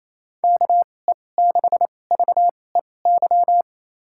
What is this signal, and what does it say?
Morse code, heard as a keyed tone, about 28 wpm, spaced as for 20 wpm, 700 Hz. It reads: KE6VEY
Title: USA Call Signs - 2x Character Spacing - 28wpm